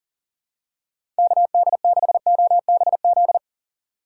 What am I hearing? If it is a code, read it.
Morse code, heard as a keyed tone, about 40 wpm, 700 Hz. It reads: KD6OBZ
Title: USA Call Signs - 6 Characters - 6x Character Spacing - 40wpm